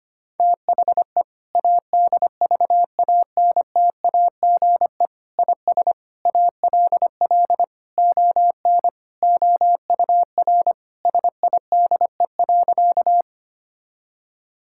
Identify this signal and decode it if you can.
Morse code, heard as a keyed tone, about 25 wpm, 700 Hz. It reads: THE ADVANTAGE IS ALL ON OUR SIDE.
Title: Sentences from Top 1000 Words - Review Part 6 - 25wpm